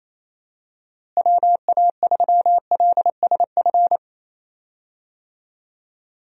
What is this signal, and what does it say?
Morse code, heard as a keyed tone, about 28 wpm, 700 Hz. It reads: WA3LSF
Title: USA Call Signs - 6 Characters - Repeated 2x - 28wpm